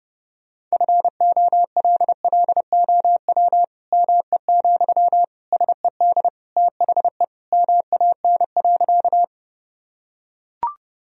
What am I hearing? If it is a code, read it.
Morse code, heard as a keyed tone, about 30 wpm, 700 Hz. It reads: FOLLOW ME, SED THE MAN.
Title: Sentences from Top 200 Words and Abbreviations 30wpm